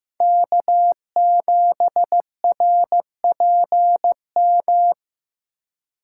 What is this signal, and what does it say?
Morse code, heard as a keyed tone, about 15 wpm, 700 Hz. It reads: K7RPM
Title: USA Call Signs - 5 Characters - 4x Character Spacing - 15wpm